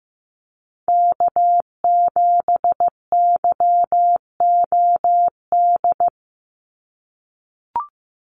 Morse code, heard as a keyed tone, about 15 wpm, 700 Hz. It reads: K7YOD
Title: USA Call Signs 15wpm